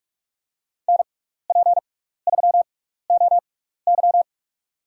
Morse code, heard as a keyed tone, about 44 wpm, spaced as for 15 wpm, 700 Hz. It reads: NP3OY
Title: USA Call Signs - 6x Character Spacing - 45wpm